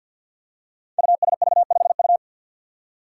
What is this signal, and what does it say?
Morse code, heard as a keyed tone, about 47 wpm, 700 Hz. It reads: USV5U
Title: Five Challenging Letter-Number Combinations 50wpm